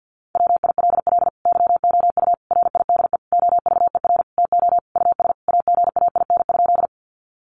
Morse code, heard as a keyed tone, about 50 wpm, 700 Hz. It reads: WILL YOU RIDE OVER TO US AGAIN?
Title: Sentences from Top 1200 Words 50wpm